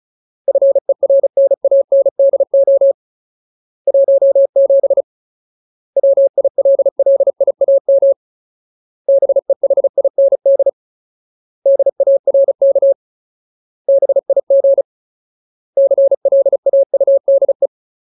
Morse code, heard as a keyed tone, about 34 wpm, 550 Hz. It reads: FERNANDO 17 WILLIAM BEHIND DARK BIG CLAUDE
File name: Ultimate Word List / 35WPM Ultimate